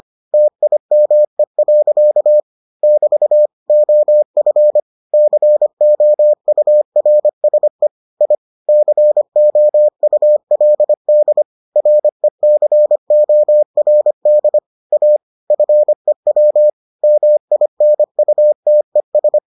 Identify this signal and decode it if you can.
Morse code, heard as a keyed tone, about 25 wpm, 600 Hz. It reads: TIME. <BT> OF COURSE I COULD RECORD A FEW MINUTES